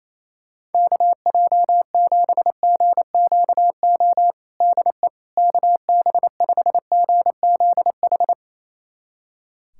Morse code, heard as a keyed tone, about 28 wpm, 700 Hz. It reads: KJ7GQO DE KB5GZH
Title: USA Callsign DE Callsign - 6 Characters - 28wpm